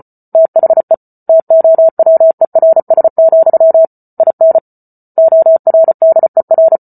Morse code, heard as a keyed tone, about 34 wpm, 650 Hz. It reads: THE TOWERS, IN ORDER